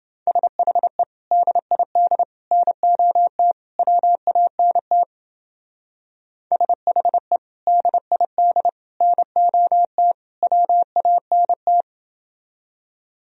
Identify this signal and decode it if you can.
Morse code, heard as a keyed tone, about 28 wpm, 700 Hz. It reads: SHE DID NOT WANT SHE DID NOT WANT